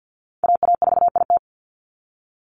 Morse code, heard as a keyed tone, about 50 wpm, 700 Hz. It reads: AA4IT
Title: NA SOTA Chaser Callsigns 50wpm